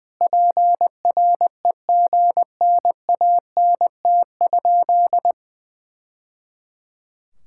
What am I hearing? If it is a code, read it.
Morse code, heard as a keyed tone, about 21 wpm, 700 Hz. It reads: PREGNANT?
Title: Sentences from Top 1000 Words - Review Part 6 - 20wpm